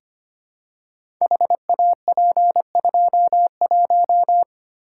Morse code, heard as a keyed tone, about 25 wpm, 700 Hz. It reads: HAP21
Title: Code Groups - 5 Letter:Numbers - 25wpm